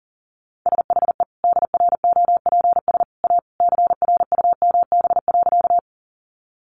Morse code, heard as a keyed tone, about 40 wpm, 700 Hz. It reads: SHE DROPS A CRUMB.
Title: Easy Phrases 40wpm